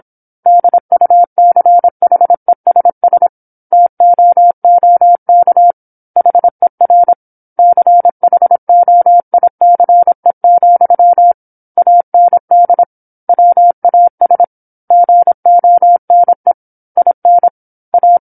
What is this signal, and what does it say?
Morse code, heard as a keyed tone, about 27 wpm, 700 Hz. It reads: DUCHESS TOOK HER CHOICE, AND WAS GONE IN A
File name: 08-Chapter-08 26 WPM